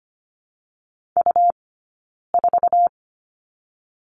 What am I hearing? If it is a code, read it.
Morse code, heard as a keyed tone, about 25 wpm, 700 Hz. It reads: U 4
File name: Lesson 17d - Letter u - Two Characters 25wpm